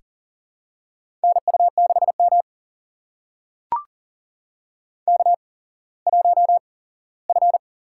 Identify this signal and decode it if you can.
Morse code, heard as a keyed tone, about 40 wpm, spaced as for 19 wpm, 700 Hz. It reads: NU6M K1F
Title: USA Call Signs - 4 Characters - 8x Character Spacing - 40wpm